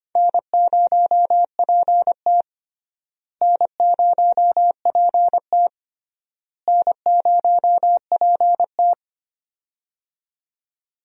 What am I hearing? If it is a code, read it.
Morse code, heard as a keyed tone, about 25 wpm, 700 Hz. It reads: N0PT N0PT N0PT
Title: USA Call Signs - 4 Characters - Repeated 3x - 25wpm